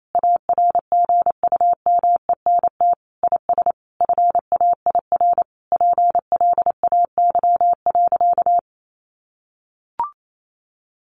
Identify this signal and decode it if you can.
Morse code, heard as a keyed tone, about 28 wpm, 700 Hz. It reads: ARGUMENT IS FAIR PLAY.
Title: Sentences from Top 1400 Words 28wpm